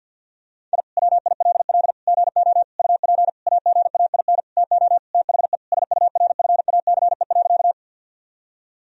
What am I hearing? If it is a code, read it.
Morse code, heard as a keyed tone, about 50 wpm, 700 Hz. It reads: I WILL GO UP AGAIN TO THE SURFACE.
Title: Sentences from Top 1000 Words - Review Part 3 - 50wpm